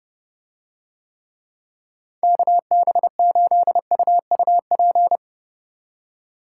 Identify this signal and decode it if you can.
Morse code, heard as a keyed tone, about 29 wpm, 700 Hz. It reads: KB8UUP